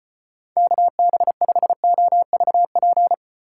Morse code, heard as a keyed tone, about 32 wpm, 700 Hz. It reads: KB5OVP